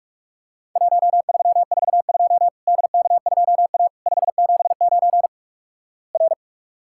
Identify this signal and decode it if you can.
Morse code, heard as a keyed tone, about 48 wpm, 700 Hz. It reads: 1342 DK2A 579 R
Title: Worked All Europe DX Contest - QTC Exchange - 45wpm